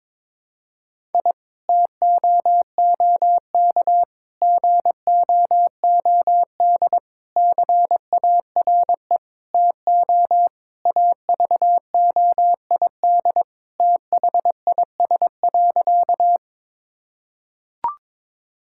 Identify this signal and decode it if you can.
Morse code, heard as a keyed tone, about 22 wpm, 700 Hz. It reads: I TOOK GOOD CARE TO AVOID THIS.
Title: Sentences from Top 1000 Words - Review Part 6 - 22wpm